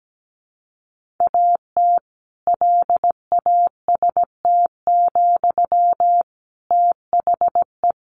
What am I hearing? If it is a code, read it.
Morse code, heard as a keyed tone, about 17 wpm, 700 Hz. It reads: AT LAST, THE
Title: Sentences from Top 1500 Words 17wpm